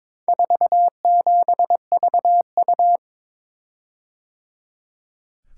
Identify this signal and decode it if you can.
Morse code, heard as a keyed tone, about 23 wpm, 700 Hz. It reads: 47VU
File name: practice / 22wpm / Four Letter-Number Combinations - 22wpm